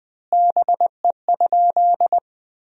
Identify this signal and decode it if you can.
Morse code, heard as a keyed tone, about 20 wpm, 700 Hz. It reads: BE?
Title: 2x Word-Spacing - Sentences from Top 200 Words - 20wpm